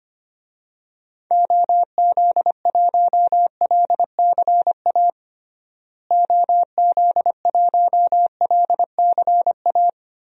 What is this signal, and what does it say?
Morse code, heard as a keyed tone, about 25 wpm, 700 Hz. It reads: OZ1LCA OZ1LCA
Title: DXCC-100 Call Signs - 6 Characters - Repeated 2x - 25wpm